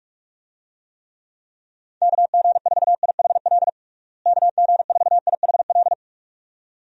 Morse code, heard as a keyed tone, about 46 wpm, 700 Hz. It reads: KG4IHL KG4IHL
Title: USA Call Signs - 6 Characters - Repeated 2x - 45wpm